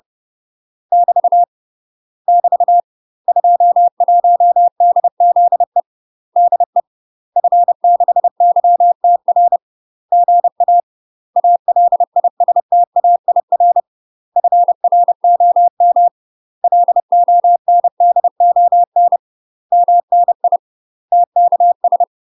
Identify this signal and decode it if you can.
Morse code, heard as a keyed tone, about 30 wpm, 700 Hz. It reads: <BT> <BT> 21DZE DE F6YTR GA ALISTAIR FROM LONDON MNI TKS
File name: UK MASTER QSOs TRAINING / 30WPM / UK MASTER QSOs 0045